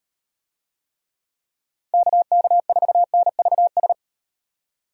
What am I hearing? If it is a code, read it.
Morse code, heard as a keyed tone, about 38 wpm, 700 Hz. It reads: KK4NVS